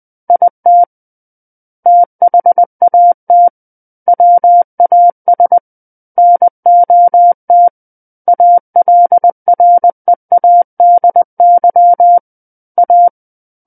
Morse code, heard as a keyed tone, about 20 wpm, 700 Hz. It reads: IT THAT WAS NOT ALREADY A
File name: 20 WPM